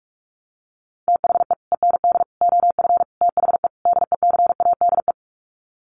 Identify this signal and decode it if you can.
Morse code, heard as a keyed tone, about 44 wpm, 700 Hz. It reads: THE END OF THE DECADE
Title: Sets of 5 Words Encore 45wpm